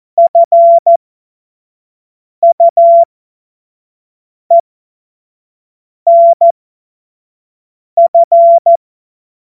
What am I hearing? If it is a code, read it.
Morse code, heard as a keyed tone, about 14 wpm, spaced as for 5 wpm, 680 Hz. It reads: FUENF
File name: Morsekurs / Witze / Witz 27. September 2022 25 F70